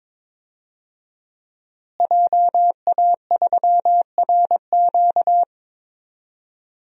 Morse code, heard as a keyed tone, about 22 wpm, 700 Hz. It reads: JA3RQ
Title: DXCC-100 Call Signs - 5 Characters - Repeated 3x - 22wpm